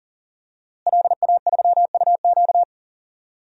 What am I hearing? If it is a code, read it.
Morse code, heard as a keyed tone, about 40 wpm, 700 Hz. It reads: LA3UQ
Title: DXCC-100 Call Signs - 5 Characters - Repeated 2x - 40wpm